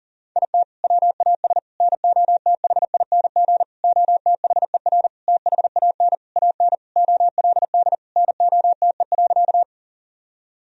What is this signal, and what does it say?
Morse code, heard as a keyed tone, about 37 wpm, 700 Hz. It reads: IT WAS NOTHING OTHER THAN AN OLD NOTE.
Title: Sentences from Top 1000 Words - Review Part 2 - 40wpm